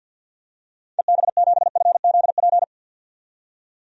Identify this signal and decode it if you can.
Morse code, heard as a keyed tone, about 51 wpm, 700 Hz. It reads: EB7FZP
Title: DXCC-100 Call Signs - 6 Characters - 50wpm